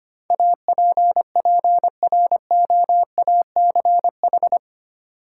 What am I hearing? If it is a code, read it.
Morse code, heard as a keyed tone, about 25 wpm, 700 Hz. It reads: APPROACH
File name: Eight Letter Words - 25wpm